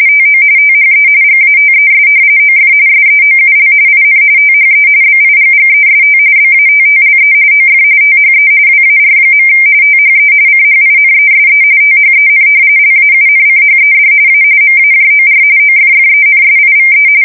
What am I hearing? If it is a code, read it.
RTTY, about 45.45 baud. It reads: TO THE GATES OF SEOUL IN THE BIGGEST BATTLE OF THE WAR. THE CHINESE THEN LAUNCHED A MAJOR OFFENSIVE
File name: fsk170 / battleshipnj